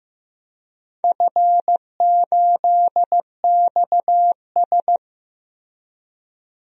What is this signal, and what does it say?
Morse code, heard as a keyed tone, about 15 wpm, 700 Hz. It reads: F8XS
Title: DXCC-100 Call Signs - 4 Characters - Repeated 3x - 15wpm